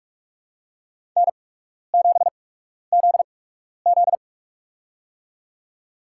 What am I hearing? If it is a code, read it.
Morse code, heard as a keyed tone, about 44 wpm, spaced as for 12 wpm, 700 Hz. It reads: N7ZZ
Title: USA Call Signs - 4 Characters - 8x Character Spacing - 45wpm